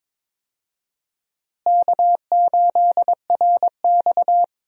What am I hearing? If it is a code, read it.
Morse code, heard as a keyed tone, about 21 wpm, 700 Hz. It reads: K8RX
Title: USA Call Signs - 4 Characters - Repeated 2x - 22wpm